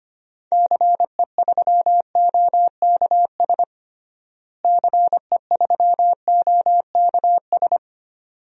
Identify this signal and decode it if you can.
Morse code, heard as a keyed tone, about 25 wpm, 700 Hz. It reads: CE3OKS CE3OKS
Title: DXCC-100 Call Signs - 6 Characters - Repeated 2x - 25wpm